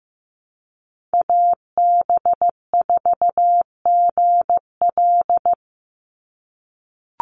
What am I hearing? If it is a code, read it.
Morse code, heard as a keyed tone, about 15 wpm, 700 Hz. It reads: AB4GL
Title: USA Call Signs 15wpm